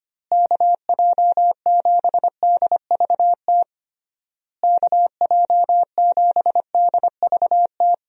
Morse code, heard as a keyed tone, about 25 wpm, 700 Hz. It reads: KJ7DVT KJ7DVT
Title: USA Call Signs - 6 Characters - Repeated 3x - 25wpm